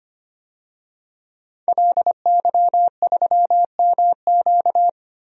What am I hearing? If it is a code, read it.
Morse code, heard as a keyed tone, about 25 wpm, 700 Hz. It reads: LY3MQ
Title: DXCC-100 Call Signs - 5 Characters - 25wpm